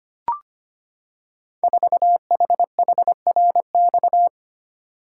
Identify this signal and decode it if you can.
Morse code, heard as a keyed tone, about 25 wpm, 700 Hz. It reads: 4HHRX
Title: Code Groups - 5 Letter:Numbers - 25wpm